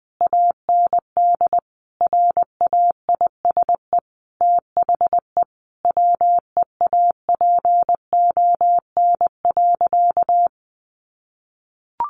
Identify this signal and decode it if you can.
Morse code, heard as a keyed tone, about 20 wpm, 700 Hz. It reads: AND RAISE THE WEAPON.
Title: A Sound Of Thunder 20wpm